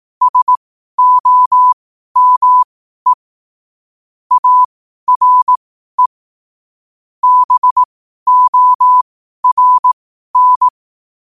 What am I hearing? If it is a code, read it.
Morse code, heard as a keyed tone, about 18 wpm, spaced as for 12 wpm, 1000 Hz. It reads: SOME ARE BORN
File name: qotdfiles / 12wpm